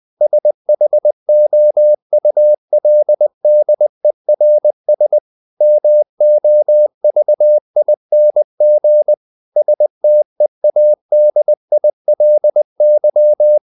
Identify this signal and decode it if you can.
Morse code, heard as a keyed tone, about 20 wpm, 600 Hz. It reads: SHOULDERS MOVING STEADILY
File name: the old man and the sea / the old man and the sea part 1 20wpm